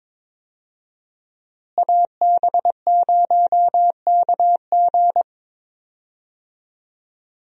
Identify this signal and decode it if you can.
Morse code, heard as a keyed tone, about 22 wpm, 690 Hz. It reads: AB0KG
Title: USA Call Signs - 5 Characters - Repeated 3x - 22wpm